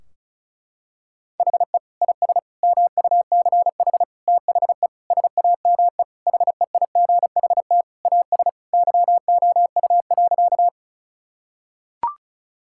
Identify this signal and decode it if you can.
Morse code, heard as a keyed tone, about 36 wpm, 700 Hz. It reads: HE IS MUCH THE SAME HEIGHT AS YOU.
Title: Sentences from Top 1600 Words 35wpm